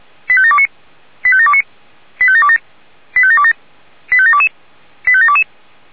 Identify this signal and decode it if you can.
Five-tone selective call (ZVEI), heard as quick runs of five tones, tones of about 70 ms. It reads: 85729 85729 85727 85727 85722 85722